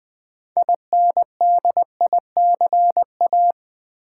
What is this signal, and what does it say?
Morse code, heard as a keyed tone, about 20 wpm, 700 Hz. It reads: INDICA
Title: Build A Word - 3 Letters At A Time - 9-Letter Words - 20wpm